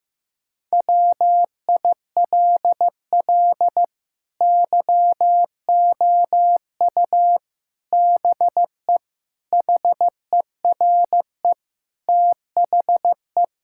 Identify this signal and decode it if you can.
Morse code, heard as a keyed tone, about 15 wpm, 700 Hz. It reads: WILL YOU BE HERE THE
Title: Sentences from Top 1000 Words - Review Part 5 - 15wpm